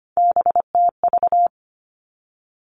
Morse code, heard as a keyed tone, about 25 wpm, 700 Hz. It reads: BTV
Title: Three Challenging Letter-Number Combinations 25wpm